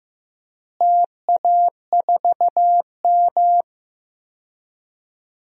Morse code, heard as a keyed tone, about 15 wpm, 700 Hz. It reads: TA4M
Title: DXCC-100 Call Signs - 4 Characters - Repeated 3x - 15wpm